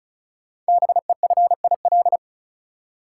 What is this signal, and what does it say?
Morse code, heard as a keyed tone, about 36 wpm, 700 Hz. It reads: BEFIL
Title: Code Groups - 5 Letters - 35wpm